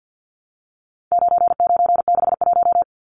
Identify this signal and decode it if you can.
Morse code, heard as a keyed tone, about 50 wpm, 700 Hz. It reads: 9961
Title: Numbers - 4-Digit 50wpm